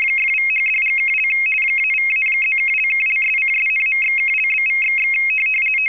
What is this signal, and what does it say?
RTTY, about 50 baud. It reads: !=#$&() -=-+()()':',./()?=//